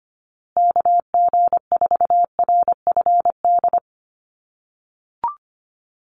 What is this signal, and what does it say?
Morse code, heard as a keyed tone, about 25 wpm, 700 Hz. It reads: KG4RFD
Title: USA Call Signs Repeated 3x 25wpm